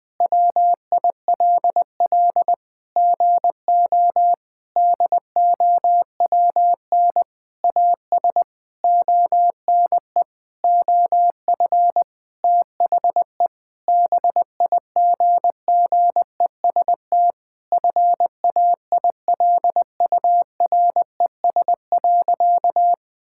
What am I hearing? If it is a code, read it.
Morse code, heard as a keyed tone, about 20 wpm, 700 Hz. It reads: WILL GO DOWN AS ONE OF THE BIGGEST FAILURES.
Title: Sentences with 7-Word Phrases - 20wpm